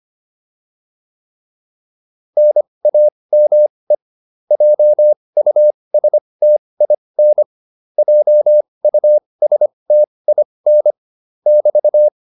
Morse code, heard as a keyed tone, about 25 wpm, spaced as for 20 wpm, 600 Hz. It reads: NAME JUSTIN JUSTIN <BT>
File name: QSO205 20